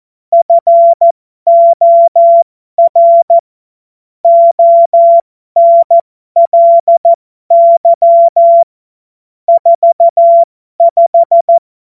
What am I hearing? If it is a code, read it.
Morse code, heard as a keyed tone, about 14 wpm, spaced as for 12 wpm, 680 Hz. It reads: FOR ONLY 45
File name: Morsekurs / Witze